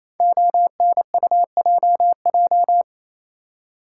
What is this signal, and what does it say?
Morse code, heard as a keyed tone, about 29 wpm, 700 Hz. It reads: ONUJJ
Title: Code Groups - 5 Letters - 28wpm